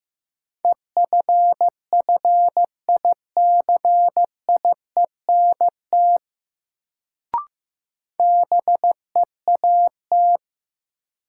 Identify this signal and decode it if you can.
Morse code, heard as a keyed tone, about 15 wpm, 700 Hz. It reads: EFFICIENT BEAT
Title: Sentences from Top 1300 Words - New Word Drill - 15wpm